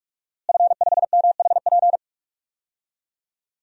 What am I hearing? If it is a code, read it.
Morse code, heard as a keyed tone, about 46 wpm, 700 Hz. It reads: F5MHP
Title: DXCC-100 Call Signs - 5 Characters - Repeated 2x - 45wpm